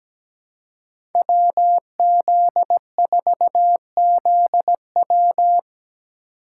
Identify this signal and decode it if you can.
Morse code, heard as a keyed tone, about 17 wpm, 700 Hz. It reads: WZ4ZW
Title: USA Call Signs - 4x Character Spacing - 17wpm